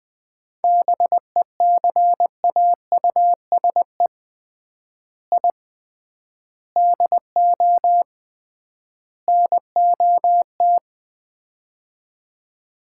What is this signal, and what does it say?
Morse code, heard as a keyed tone, about 21 wpm, 700 Hz. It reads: BECAUSE I DO NOT
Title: 3x Word-Spacing - Sets of 4 Words - 20wpm